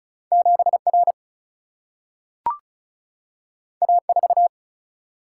Morse code, heard as a keyed tone, about 36 wpm, 700 Hz. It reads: 7 R A 4
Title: Single-Letter Single-Number 35wpm